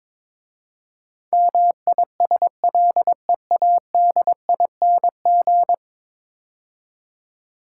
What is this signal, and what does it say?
Morse code, heard as a keyed tone, about 22 wpm, 700 Hz. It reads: MISLEADING
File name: practice / 22wpm / 10 Letter Words - First 3 Letters Given - 22wpm